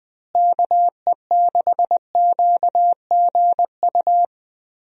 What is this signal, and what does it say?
Morse code, heard as a keyed tone, about 20 wpm, 700 Hz. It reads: KE6QGU
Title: USA Call Signs - 6 Characters - Repeated 3x - 20wpm